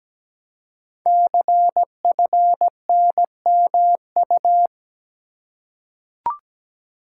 Morse code, heard as a keyed tone, about 17 wpm, 700 Hz. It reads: CFNMU
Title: Code Groups - 5 Letters - 17wpm